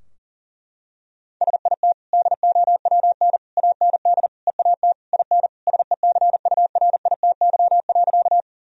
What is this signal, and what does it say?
Morse code, heard as a keyed tone, about 40 wpm, 700 Hz. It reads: SIT DOWN AND EAT IN SECURITY.
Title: Sentences from Top 1000 Words - Review Part 4 - 40wpm